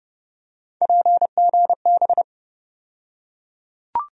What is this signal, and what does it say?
Morse code, heard as a keyed tone, about 29 wpm, 700 Hz. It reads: PGB
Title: USA Call Sign Postfixes - 3 Letters 30wpm